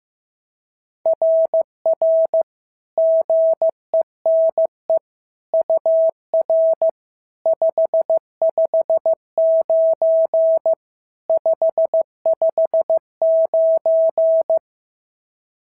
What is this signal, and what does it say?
Morse code, heard as a keyed tone, about 14 wpm, 650 Hz. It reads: RR GENE UR 559 559
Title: SOTA QSO Practice 15wpm